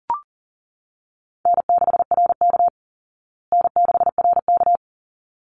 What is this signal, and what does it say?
Morse code, heard as a keyed tone, about 40 wpm, 700 Hz. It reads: N6RK N6RK
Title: Easy Call Signs Repeated 2x - 40wpm